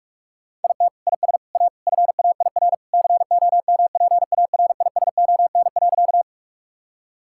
Morse code, heard as a keyed tone, about 46 wpm, 700 Hz. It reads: IT IS A FAIR COMPARISON.
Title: Sentences from Top 1600 Words 45wpm